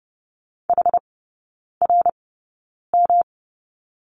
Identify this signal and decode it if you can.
Morse code, heard as a keyed tone, about 30 wpm, spaced as for 9 wpm, 700 Hz. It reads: HRM
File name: Lesson 19e - Letter w - Three Characters 30wpm